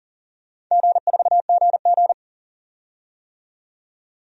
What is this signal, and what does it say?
Morse code, heard as a keyed tone, about 40 wpm, 700 Hz. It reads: G4GG
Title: DXCC-100 Call Signs - 4 Characters - 40wpm